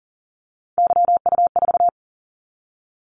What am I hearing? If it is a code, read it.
Morse code, heard as a keyed tone, about 42 wpm, 700 Hz. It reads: YU4
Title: Callsign Prefixes for Top 25 DXCC - 40wpm